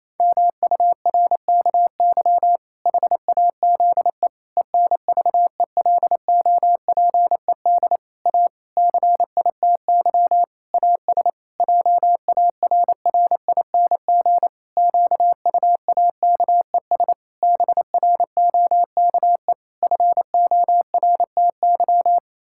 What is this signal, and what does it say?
Morse code, heard as a keyed tone, about 28 wpm, 700 Hz. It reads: MURKY HAZE ENVELOPED A CITY AS JARRING QUAKES BROKE FORTY
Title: Pangram Challenge v2 - 28wpm